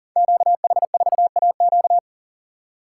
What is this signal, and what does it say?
Morse code, heard as a keyed tone, about 40 wpm, 700 Hz. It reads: QH4AQ
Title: Code Groups - 5 Letter:Numbers - 40wpm